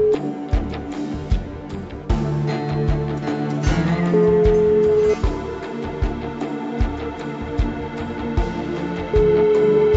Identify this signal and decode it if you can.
Ringback tone (Europe, 425 Hz), with noise.